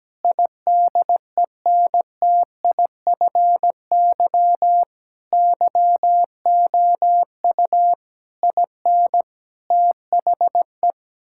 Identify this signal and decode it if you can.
Morse code, heard as a keyed tone, about 17 wpm, 700 Hz. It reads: IDENTIFY YOU IN THE
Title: Sentences from Top 1000 Words - Review Part 6 - 17wpm